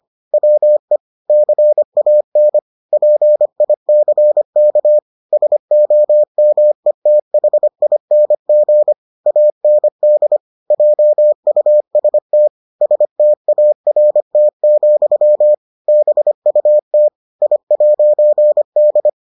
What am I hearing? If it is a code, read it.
Morse code, heard as a keyed tone, about 25 wpm, 600 Hz. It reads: WE CAN PICK SOMETHING AND JUST START, BUT I'D